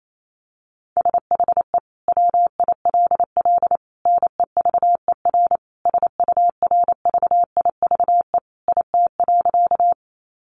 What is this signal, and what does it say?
Morse code, heard as a keyed tone, about 28 wpm, 700 Hz. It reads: SHE WILL NEVER SURVIVE IT.
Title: Sentences from Top 1300 Words 28wpm